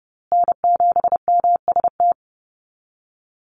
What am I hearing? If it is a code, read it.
Morse code, heard as a keyed tone, about 30 wpm, 700 Hz. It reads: N7MST
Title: USA Call Signs Repeated 2x 30wpm